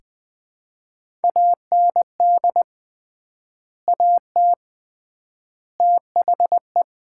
Morse code, heard as a keyed tone, about 20 wpm, 690 Hz. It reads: AND AT THE
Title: 3x Word-Spacing - Sets of 5 Words - 20wpm